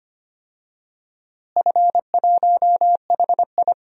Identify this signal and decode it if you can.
Morse code, heard as a keyed tone, about 25 wpm, 700 Hz. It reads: F1HI
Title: DXCC-100 Call Signs - 4 Characters - Repeated 3x - 25wpm